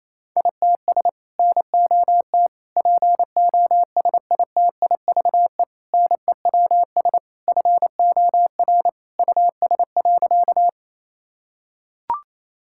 Morse code, heard as a keyed tone, about 28 wpm, 700 Hz. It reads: ITS NOT POSITIVE NEWS FOR US.
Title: Sentences from Top 1000 Words - Review Part 5 - 28wpm